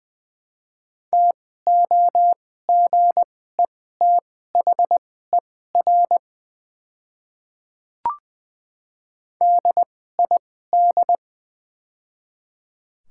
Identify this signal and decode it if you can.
Morse code, heard as a keyed tone, about 20 wpm, spaced as for 15 wpm, 700 Hz. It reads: TOGETHER DID
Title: 2x Character Spacing - Top 300 Words - 20wpm